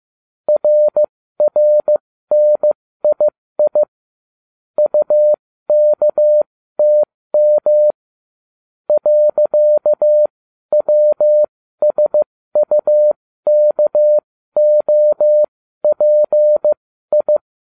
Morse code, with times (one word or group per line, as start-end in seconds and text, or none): 0.5-3.8 RRNII
4.8-7.9 UKTM
8.9-17.4 .WSUKOPI